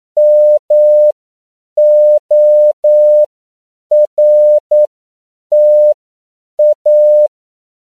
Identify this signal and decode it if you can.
Morse code, heard as a keyed tone, about 9 wpm, spaced as for 7 wpm, 600 Hz. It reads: MORTA
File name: CW / 10PPM